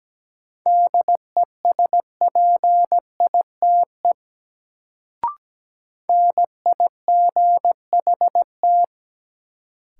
Morse code, heard as a keyed tone, about 17 wpm, 700 Hz. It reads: DESPITE NIGHT
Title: Sets of 4 Words - New Word Drill - 17wpm